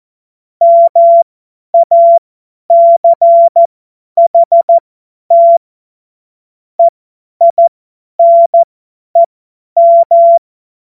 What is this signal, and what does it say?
Morse code, heard as a keyed tone, about 14 wpm, spaced as for 10 wpm, 680 Hz. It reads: MACHT EINEM
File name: Morsekurs / Witze / Witz 27 Februar 2024 50 F70